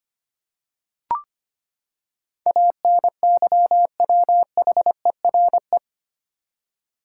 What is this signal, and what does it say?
Morse code, heard as a keyed tone, about 25 wpm, 700 Hz. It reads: ANYWHERE
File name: Eight Letter Words - 25wpm